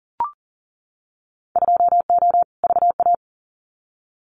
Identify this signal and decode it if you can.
Morse code, heard as a keyed tone, about 38 wpm, 700 Hz. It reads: 2O VA
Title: Winter Field Day Exchanges 40wpm